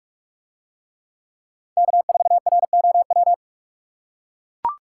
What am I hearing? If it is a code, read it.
Morse code, heard as a keyed tone, about 44 wpm, 700 Hz. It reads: K4ROW